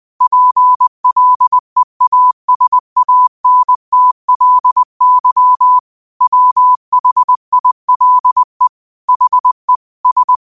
Morse code, heard as a keyed tone, about 20 wpm, 1000 Hz. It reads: PLEASANTLY WHILE HE S
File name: qotdfiles / 20wpm